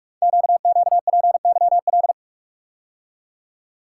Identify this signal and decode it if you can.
Morse code, heard as a keyed tone, about 44 wpm, 700 Hz. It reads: QQPYL
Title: Code Groups - 5 Letters - 45wpm